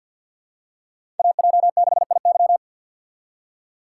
Morse code, heard as a keyed tone, about 53 wpm, 700 Hz. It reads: AJ6IY
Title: USA Call Signs - 5 Characters - 4x Character Spacing - 50wpm